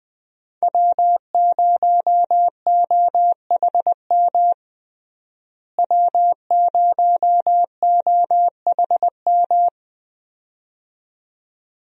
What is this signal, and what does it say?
Morse code, heard as a keyed tone, about 20 wpm, 700 Hz. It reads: W0OHM W0OHM
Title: USA Call Signs - 5 Characters - Repeated 2x - 20wpm